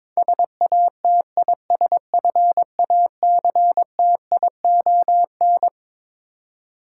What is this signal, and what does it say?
Morse code, heard as a keyed tone, about 22 wpm, 700 Hz. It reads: SATISFACTION
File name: Twelve Letter Words - 22wpm